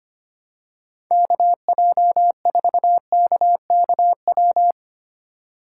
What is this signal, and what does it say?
Morse code, heard as a keyed tone, about 25 wpm, 700 Hz. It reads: KJ4KKW
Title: USA Call Signs - 6 Characters - 3x Character Spacing - 25wpm